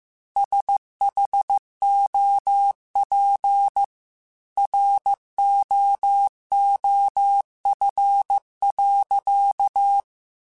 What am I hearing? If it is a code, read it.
Morse code, heard as a keyed tone, about 15 wpm, 800 Hz. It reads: SHOP ROOF.
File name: Weather Sets / Weather Sets-mp3s / Weather 4 24 @13